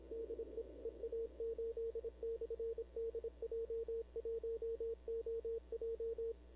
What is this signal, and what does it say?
Morse code, heard as a keyed tone, about 26 wpm, 460 Hz. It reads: DE EA8/DJ1OJ
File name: arrl cw dx 2005